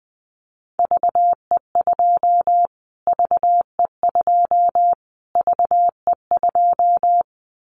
Morse code, heard as a keyed tone, about 20 wpm, 700 Hz. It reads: VE2 VE2 VE2
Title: Callsign Prefixes for Top 25 DXCC - Repeated 3x - 20wpm